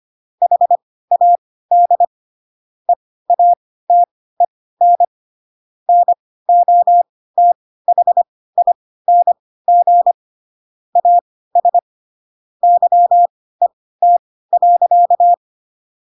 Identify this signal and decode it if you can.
Morse code, heard as a keyed tone, about 25 wpm, spaced as for 16 wpm, 700 Hz. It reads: HAD EATEN NOTHING AS YET.